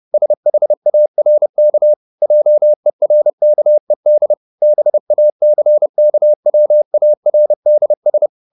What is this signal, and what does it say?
Morse code, heard as a keyed tone, about 30 wpm, 600 Hz. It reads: SHARK JERKED BACKWARDS